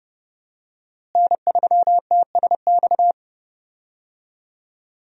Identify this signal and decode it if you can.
Morse code, heard as a keyed tone, about 30 wpm, 700 Hz. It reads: N3TSX
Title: USA Call Signs - 5 Characters - Repeated 2x - 30wpm